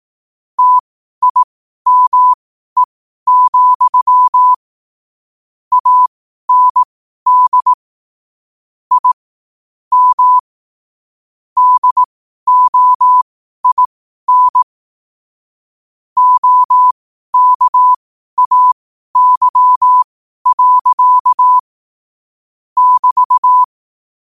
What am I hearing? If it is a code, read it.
Morse code, heard as a keyed tone, about 18 wpm, spaced as for 12 wpm, 1000 Hz. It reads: TIME, AND I M DOIN OKAY. <BT>
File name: qotdfiles / 12wpm